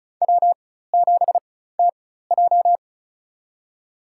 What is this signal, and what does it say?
Morse code, heard as a keyed tone, about 36 wpm, spaced as for 16 wpm, 700 Hz. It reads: W7TJ
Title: USA Call Signs - 4 Characters - 4x Character Spacing - 35wpm